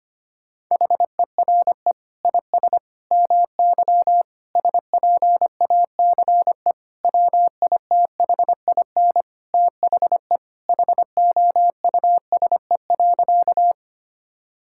Morse code, heard as a keyed tone, about 25 wpm, 700 Hz. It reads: HERE IS MY SPACE WITHIN THE HOUSE.
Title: Sentences from Top 1000 Words - Review Part 5 - 25wpm